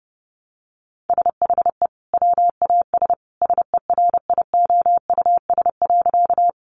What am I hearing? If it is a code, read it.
Morse code, heard as a keyed tone, about 30 wpm, 700 Hz. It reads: SHE WAS SERIOUS.
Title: Sentences from Top 1100 Words 30wpm